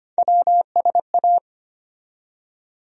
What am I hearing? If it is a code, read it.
Morse code, heard as a keyed tone, about 25 wpm, 700 Hz. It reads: WSA
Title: USA Call Sign Postfixes - 3 Letters 25wpm